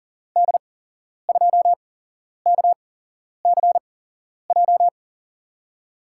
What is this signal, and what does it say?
Morse code, heard as a keyed tone, about 40 wpm, spaced as for 11 wpm, 700 Hz. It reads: D2KCJ
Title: USA Call Signs - 6 Characters - 8x Character Spacing - 40wpm